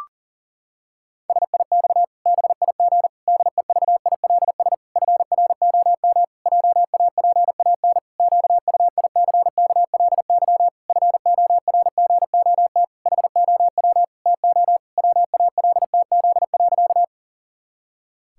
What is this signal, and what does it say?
Morse code, heard as a keyed tone, about 40 wpm, 700 Hz. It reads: SIX BIG DEVILS FROM JAPAN QUICKLY FORGOT HOW TO WALTZ.
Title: Pangram Challenge v2 - 40wpm